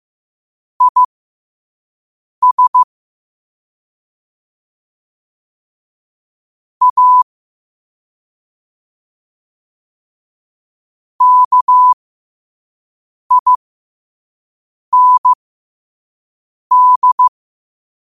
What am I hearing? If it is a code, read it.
Morse code, heard as a keyed tone, about 15 wpm, spaced as for 5 wpm, 1000 Hz. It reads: IS A KIND
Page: New quotes every day in morse code at 5 Words per minute.